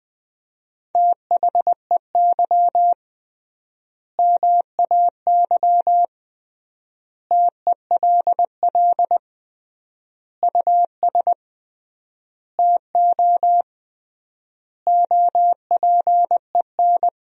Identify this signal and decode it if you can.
Morse code, heard as a keyed tone, about 20 wpm, 700 Hz. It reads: THEY MAY TELL US TO OPEN
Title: 3x Word-Spacing - Sentences from Top 300 Words - 20wpm